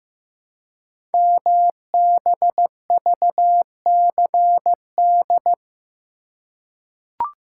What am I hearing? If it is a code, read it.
Morse code, heard as a keyed tone, about 15 wpm, 700 Hz. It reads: MBVCD
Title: Code Groups - 5 Letters - 15wpm